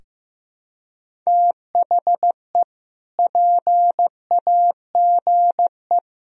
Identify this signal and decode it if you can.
Morse code, heard as a keyed tone, about 15 wpm, 700 Hz. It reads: THE PAGE
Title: Sentences from Top 1000 Words - Review Part 1 - 15wpm